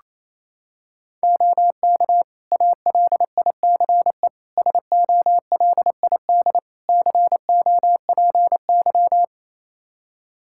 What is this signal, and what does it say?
Morse code, heard as a keyed tone, about 29 wpm, 700 Hz. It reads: OK ALICE SOLID COPY
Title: Practice QSO - Exchange 3 - More Info - 28wpm